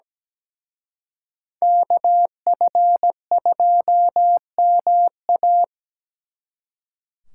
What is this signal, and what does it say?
Morse code, heard as a keyed tone, about 20 wpm, 700 Hz. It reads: KF2MA
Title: Speed Racing - USA 2x2 Call Signs 17wpm